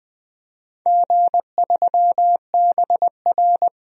Morse code, heard as a keyed tone, about 20 wpm, 700 Hz. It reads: G3BR
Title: DXCC-100 Call Signs - 4 Characters - Repeated 3x - 20wpm